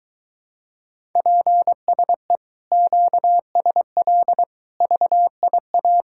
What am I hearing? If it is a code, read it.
Morse code, heard as a keyed tone, about 23 wpm, 700 Hz. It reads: PSE QSL VIA